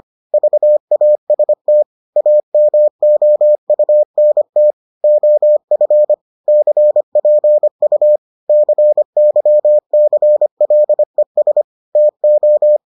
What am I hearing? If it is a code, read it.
Morse code, heard as a keyed tone, about 25 wpm, 600 Hz. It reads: VAST AMOUNT OF CPU CYCLES TO